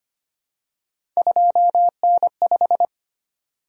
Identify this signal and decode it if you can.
Morse code, heard as a keyed tone, about 25 wpm, 700 Hz. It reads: 2N5
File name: Three Letter-Number Combinations - 25wpm